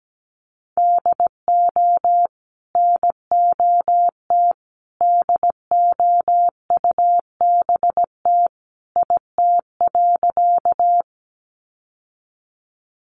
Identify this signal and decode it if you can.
Morse code, heard as a keyed tone, about 17 wpm, 700 Hz. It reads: DO NOT DOUBT IT.
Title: Sentences from Top 1500 Words 17wpm